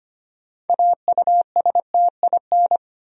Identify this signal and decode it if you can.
Morse code, heard as a keyed tone, about 25 wpm, 700 Hz. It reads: AUSTIN
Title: USA Common Names - 25wpm